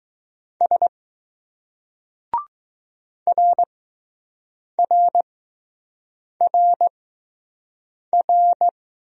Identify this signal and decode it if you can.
Morse code, heard as a keyed tone, about 20 wpm, spaced as for 6 wpm, 700 Hz. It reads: S RRRR